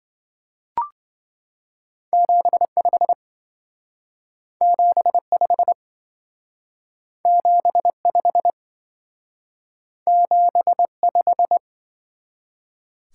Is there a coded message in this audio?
Yes: Morse code from 2.1 to 11.6 s, heard as a keyed tone, about 25 wpm, 700 Hz. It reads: 75 75 75 75